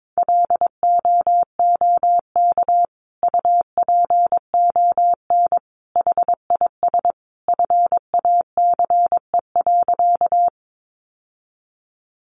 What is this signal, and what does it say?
Morse code, heard as a keyed tone, about 22 wpm, 700 Hz. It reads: LOOK UPON HIS FACE.
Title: Sentences from Top 1100 Words 22wpm